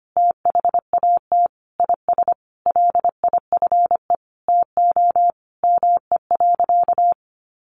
Morse code, heard as a keyed tone, about 25 wpm, 700 Hz. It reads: THAT IS LIFE TO ME.
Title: Sentences from Top 500 Words 25wpm